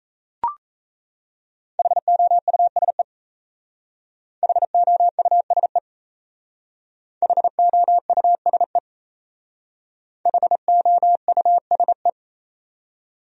Morse code, heard as a keyed tone, about 34 wpm, 700 Hz. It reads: HOUSE HOUSE HOUSE HOUSE
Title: Speed-Racing Top 500 Words 28wpm